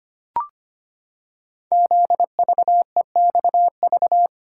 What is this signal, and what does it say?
Morse code, heard as a keyed tone, about 25 wpm, 700 Hz. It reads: ZVEXV
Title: Code Groups - 5 Letters - 25wpm